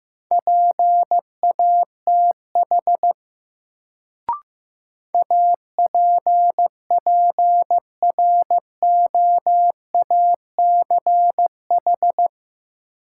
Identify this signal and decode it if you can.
Morse code, heard as a keyed tone, about 15 wpm, 700 Hz. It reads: PATH APPROACH
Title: Sentences from Top 1400 Words - New Word Drill Alt - 15wpm